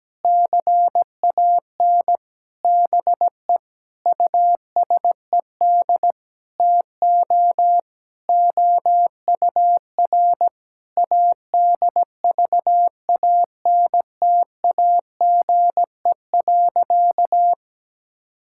Morse code, heard as a keyed tone, about 17 wpm, 700 Hz. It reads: CAN BE USED TO OUR ADVANTAGE.
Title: Sentences from Top 1000 Words - Review Part 6 - 17wpm